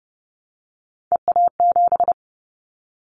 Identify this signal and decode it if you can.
Morse code, heard as a keyed tone, about 30 wpm, 700 Hz. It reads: EA7
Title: Callsign Prefixes for Top 25 DXCC - Repeated 3x - 30wpm